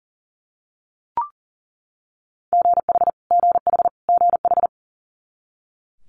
Morse code, heard as a keyed tone, about 37 wpm, 700 Hz. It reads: GH GH GH
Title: Multi-letter Phonemes Repeated 3x 40wpm